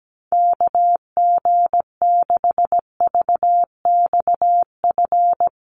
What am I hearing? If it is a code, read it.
Morse code, heard as a keyed tone, about 17 wpm, 700 Hz. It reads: KG6VXF
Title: USA Call Signs Repeated 2x 17wpm